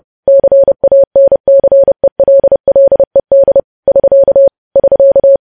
Morse code, heard as a keyed tone, about 30 wpm, 550 Hz. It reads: CANCELLED <SK> <SK>
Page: news headlines in morse code at 30 wpm